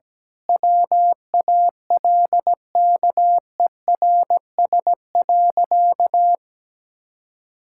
Morse code, heard as a keyed tone, about 17 wpm, 700 Hz. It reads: WALKERS.
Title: Pangram Challenge v2 - 17wpm